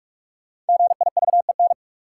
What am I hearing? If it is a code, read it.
Morse code, heard as a keyed tone, about 42 wpm, 700 Hz. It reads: GIVEN
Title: Sentences from Top 1400 Words - New Word Drill - 45wpm